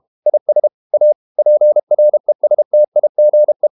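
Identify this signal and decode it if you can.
Morse code, heard as a keyed tone, about 32 wpm, 600 Hz. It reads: IS A PRESTIGE
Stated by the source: Morse code practice at 32 wpm from CW med Gnister, reading Ken Follett's novel Never in English.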